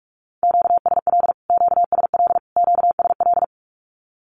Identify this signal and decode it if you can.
Morse code, heard as a keyed tone, about 44 wpm, 700 Hz. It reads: QSL QSL QSL
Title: Q-Codes Repeated 3x 45wpm